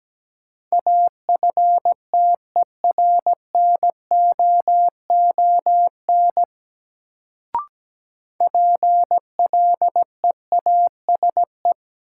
Morse code, heard as a keyed tone, about 17 wpm, 700 Hz. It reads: AFTERNOON PLEASE
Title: Sentences from Top 1500 Words - New Word Drill - 17wpm